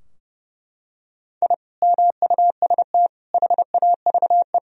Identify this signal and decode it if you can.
Morse code, heard as a keyed tone, about 30 wpm, 700 Hz. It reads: I MUST HAVE
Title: Sentences from Top 1000 Words - Review Part 4 - 30wpm